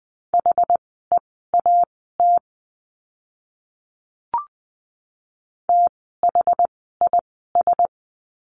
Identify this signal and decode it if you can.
Morse code, heard as a keyed tone, about 20 wpm, spaced as for 15 wpm, 700 Hz. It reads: HEAT THIS
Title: 061 Lesson 11f - Letter h - Words 20wpm